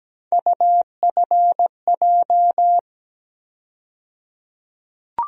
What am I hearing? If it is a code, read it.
Morse code, heard as a keyed tone, about 17 wpm, 700 Hz. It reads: UFJ
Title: USA Call Sign Postfixes - 3 Letters 17wpm